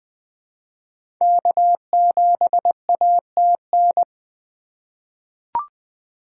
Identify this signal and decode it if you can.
Morse code, heard as a keyed tone, about 21 wpm, 700 Hz. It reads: K7ATN
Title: 2023 SOTA Chaser Call Signs - 20wpm